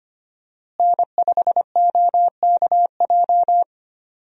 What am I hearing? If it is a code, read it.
Morse code, heard as a keyed tone, about 25 wpm, 700 Hz. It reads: N5OKJ
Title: USA Call Signs - 5 Characters - Repeated 2x - 25wpm